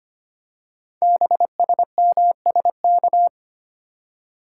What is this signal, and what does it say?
Morse code, heard as a keyed tone, about 25 wpm, 700 Hz. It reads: BSMSK
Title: Code Groups - 5 Letters - 25wpm